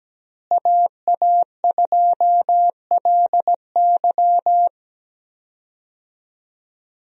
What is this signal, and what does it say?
Morse code, heard as a keyed tone, about 17 wpm, 700 Hz. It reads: AA2LY
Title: USA Call Signs - 5 Characters - Repeated 3x - 17wpm